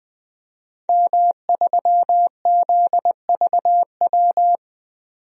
Morse code, heard as a keyed tone, about 20 wpm, 700 Hz. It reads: M3ZVW
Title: DXCC-100 Call Signs - 5 Characters - 20wpm